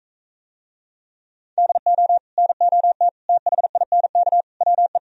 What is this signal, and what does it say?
Morse code, heard as a keyed tone, about 45 wpm, 700 Hz. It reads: DO NOT THINK WE